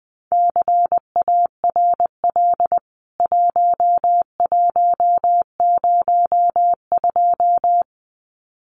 Morse code, heard as a keyed tone, about 20 wpm, 700 Hz. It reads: CARL 1102
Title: CWT Exchange 20wpm